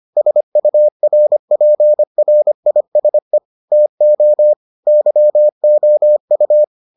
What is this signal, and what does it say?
Morse code, heard as a keyed tone, about 25 wpm, 600 Hz. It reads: SURPRISE TO YOU